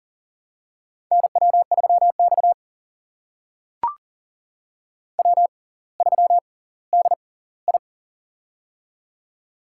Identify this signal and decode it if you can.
Morse code, heard as a keyed tone, about 40 wpm, spaced as for 20 wpm, 700 Hz. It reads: NW3X W3DI
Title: USA Call Signs - 4 Characters - 6x Character Spacing - 40wpm